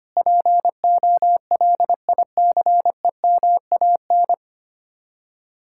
Morse code, heard as a keyed tone, about 32 wpm, 700 Hz. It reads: POLICEMAN